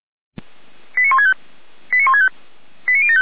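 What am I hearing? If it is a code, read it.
Five-tone selective call (ZVEI), heard as quick runs of five tones, tones of about 70 ms. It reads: 89165 89165 89075
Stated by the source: Received on SDR, ZVEI five-tone selective call 89560.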